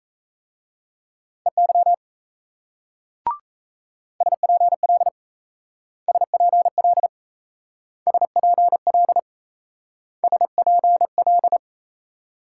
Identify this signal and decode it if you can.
Morse code, heard as a keyed tone, about 36 wpm, 700 Hz. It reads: EY SPL SPL SPL SPL